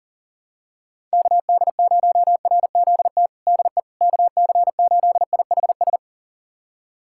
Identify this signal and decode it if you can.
Morse code, heard as a keyed tone, about 37 wpm, 700 Hz. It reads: KD0RZT DE KC8IHS
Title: USA Callsign DE Callsign 40wpm